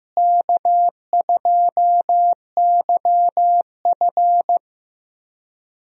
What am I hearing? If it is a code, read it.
Morse code, heard as a keyed tone, about 16 wpm, 700 Hz. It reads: K2YF
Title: USA Call Signs - 4 Characters - Repeated 2x - 15wpm